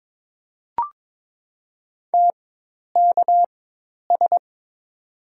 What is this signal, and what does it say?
Morse code, heard as a keyed tone, about 22 wpm, 700 Hz. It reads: T K S
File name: Top 90 QSO Elements - 4x Character Spacing - 22wpm